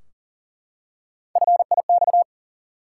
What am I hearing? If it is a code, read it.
Morse code, heard as a keyed tone, about 40 wpm, 700 Hz. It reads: FIX
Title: Three Letter Words - 40wpm